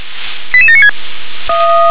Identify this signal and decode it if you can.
Five-tone selective call (ZVEI), heard as quick runs of five tones, tones of about 70 ms. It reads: 88796